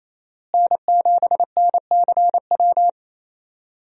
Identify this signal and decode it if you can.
Morse code, heard as a keyed tone, about 28 wpm, 700 Hz. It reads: N7NCW
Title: USA Call Signs - 5 Characters - Repeated 2x - 28wpm